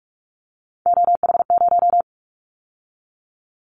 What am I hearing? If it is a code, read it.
Morse code, heard as a keyed tone, about 46 wpm, 700 Hz. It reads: OH0
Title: Callsign Prefixes for Top 100 DXCC - 45wpm